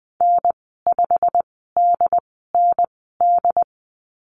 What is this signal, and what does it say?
Morse code, heard as a keyed tone, about 19 wpm, spaced as for 14 wpm, 700 Hz. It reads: N5DND
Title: 147 Lesson 24g - Letter y - Call Signs 20wpm